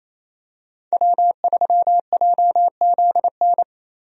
Morse code, heard as a keyed tone, about 28 wpm, 700 Hz. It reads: W3JZN
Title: USA Call Signs - 5 Characters - Repeated 2x - 28wpm